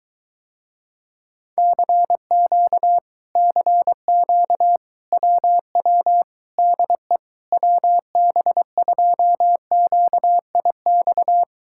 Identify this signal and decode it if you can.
Morse code, heard as a keyed tone, about 24 wpm, 700 Hz. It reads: CQ CQ WW DE WB2QIX